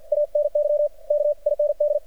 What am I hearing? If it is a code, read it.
Morse code, heard as a keyed tone, about 48 wpm, 590 Hz. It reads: ANY K IND
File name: archived / CRSnet